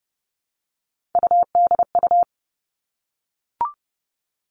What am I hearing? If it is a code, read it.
Morse code, heard as a keyed tone, about 30 wpm, 700 Hz. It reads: UDU
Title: Three Challenging Letter-Number Combinations 30wpm